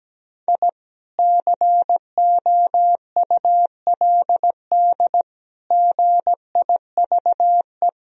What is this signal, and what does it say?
Morse code, heard as a keyed tone, about 17 wpm, 700 Hz. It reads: I COULD GIVE
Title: Sentences from Top 1000 Words - Review Part 1 - 17wpm